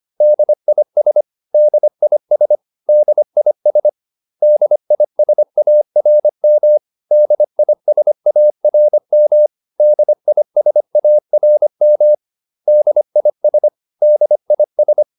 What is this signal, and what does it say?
Morse code, heard as a keyed tone, about 25 wpm, 600 Hz. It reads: DIS DIS DIS DISARM DISARM DISARM DIS DIS